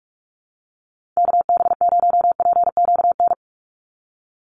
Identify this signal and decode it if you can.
Morse code, heard as a keyed tone, about 46 wpm, 700 Hz. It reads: KB0PQN
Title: USA Call Signs 45wpm